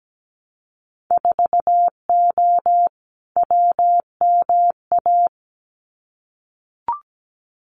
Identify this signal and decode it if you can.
Morse code, heard as a keyed tone, about 17 wpm, 700 Hz. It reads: 4O WMA
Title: Winter Field Day Exchanges Encore 17wpm